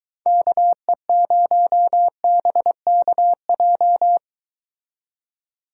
Morse code, heard as a keyed tone, about 23 wpm, 700 Hz. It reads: KE0BKJ